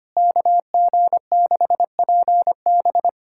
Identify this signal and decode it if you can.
Morse code, heard as a keyed tone, about 25 wpm, 700 Hz. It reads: KG6PB
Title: K1USN SST QSO Practice - 25wpm